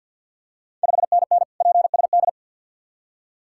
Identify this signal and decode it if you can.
Morse code, heard as a keyed tone, about 51 wpm, 700 Hz. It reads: 5NN PSD
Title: Ontario QSO Party Exchange 50wpm